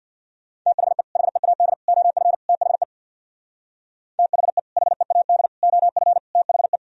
Morse code, heard as a keyed tone, about 66 wpm, 700 Hz. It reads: THE HEAD OF THE THE HEAD OF THE